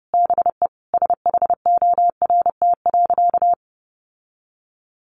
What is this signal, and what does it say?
Morse code, heard as a keyed tone, about 30 wpm, 700 Hz. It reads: BE SHORT.
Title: Sentences from Top 1400 Words 30wpm